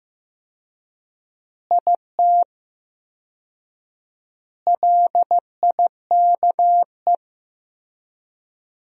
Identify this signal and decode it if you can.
Morse code, heard as a keyed tone, about 15 wpm, 700 Hz. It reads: IT LIKE
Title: 4x Word-Spacing - Sentences from Top 300 Words - 15wpm